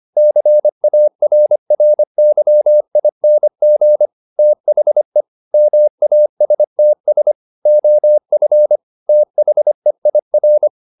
Morse code, heard as a keyed tone, about 25 wpm, 600 Hz. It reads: CARRYING THE MASTS OF THEIR
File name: the old man and the sea part 1 25wpm